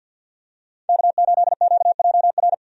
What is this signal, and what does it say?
Morse code, heard as a keyed tone, about 52 wpm, 700 Hz. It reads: K8QJR
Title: USA Call Signs - 5 Characters - 50wpm